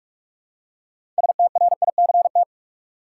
Morse code, heard as a keyed tone, about 47 wpm, 700 Hz. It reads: STRICT
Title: Six Letter Words - 45wpm